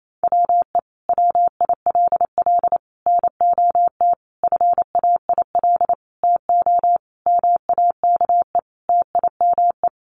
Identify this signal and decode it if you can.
Morse code, heard as a keyed tone, about 28 wpm, 700 Hz. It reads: WE WILL NOT FAIL TO MAKE TIME
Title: Sentences from Top 1100 Words 28wpm